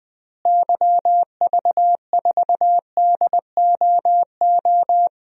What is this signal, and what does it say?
Morse code, heard as a keyed tone, about 21 wpm, 700 Hz. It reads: YV4DOO
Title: DXCC-100 Call Signs - 6 Characters - Repeated 3x - 20wpm